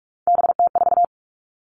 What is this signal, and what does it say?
Morse code, heard as a keyed tone, about 44 wpm, 700 Hz. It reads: BT4
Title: All Cut Numbers - 3-Digit 45wpm